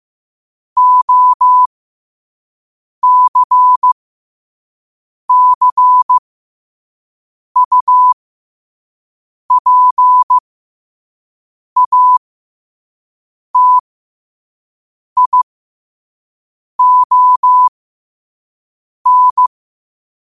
Morse code, heard as a keyed tone, about 15 wpm, spaced as for 5 wpm, 1000 Hz. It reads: OCCUPATION
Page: New quotes every day in morse code at 5 Words per minute.